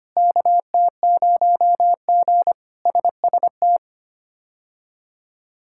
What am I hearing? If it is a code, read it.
Morse code, heard as a keyed tone, about 25 wpm, 700 Hz. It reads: KT0G SST
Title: K1USN SST QSO Practice - 25wpm